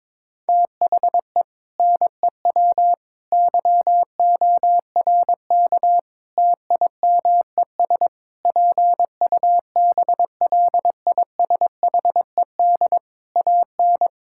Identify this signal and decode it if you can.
Morse code, heard as a keyed tone, about 22 wpm, 700 Hz. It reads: THE NEW YORK TIMES PUBLISHED AN
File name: practice / 22wpm / Sentences with 4-Word Phrases - 22wpm